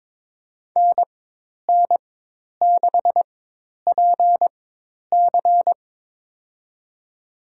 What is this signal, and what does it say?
Morse code, heard as a keyed tone, about 22 wpm, spaced as for 10 wpm, 700 Hz. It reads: NN6PC
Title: USA Call Signs - 5 Characters - 4x Character Spacing - 22wpm